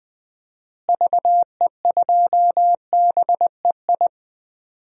Morse code, heard as a keyed tone, about 20 wpm, 700 Hz. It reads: VE2BEI
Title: DXCC-100 Call Signs - 6 Characters - Repeated 3x - 20wpm